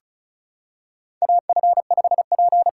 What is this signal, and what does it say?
Morse code, heard as a keyed tone, about 36 wpm, 700 Hz. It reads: AF5P
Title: K1USN SST QSO Practice - 35wpm